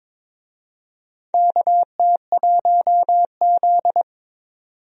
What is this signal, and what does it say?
Morse code, heard as a keyed tone, about 22 wpm, 700 Hz. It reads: KT1Z
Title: USA Call Signs - 4 Characters - 22wpm